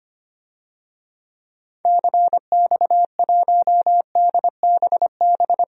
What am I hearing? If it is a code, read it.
Morse code, heard as a keyed tone, about 25 wpm, 700 Hz. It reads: CX1DBB
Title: DXCC-100 Call Signs - 6 Characters - Repeated 2x - 25wpm